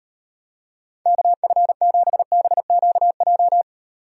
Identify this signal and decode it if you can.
Morse code, heard as a keyed tone, about 38 wpm, 700 Hz. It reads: KF7BQJ